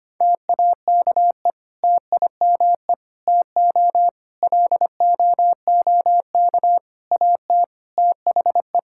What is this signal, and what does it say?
Morse code, heard as a keyed tone, about 25 wpm, 700 Hz. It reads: TAKE TIME TO LOOK AT THE
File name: practice / 25wpm / Sentences with 4-Word Phrases - 25wpm